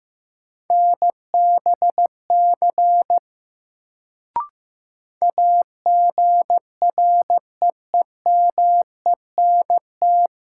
Morse code, heard as a keyed tone, about 15 wpm, 700 Hz. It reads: NBC AGREEMENT
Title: Sets of 5 Words Encore - New Word Drill - 15wpm